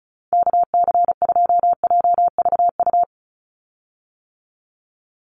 Morse code, heard as a keyed tone, about 34 wpm, 700 Hz. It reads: KC2JVU
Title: USA Call Signs 35wpm